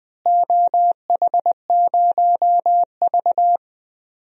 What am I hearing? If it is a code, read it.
Morse code, heard as a keyed tone, about 21 wpm, 700 Hz. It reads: OH0V
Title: DXCC-100 Call Signs - 4 Characters - Repeated 2x - 20wpm